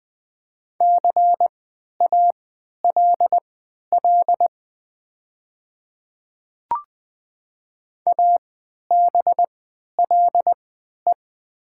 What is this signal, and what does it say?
Morse code, heard as a keyed tone, about 20 wpm, spaced as for 11 wpm, 700 Hz. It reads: CALL ABLE
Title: 3x Character Spacing - Top 500 Words - 20wpm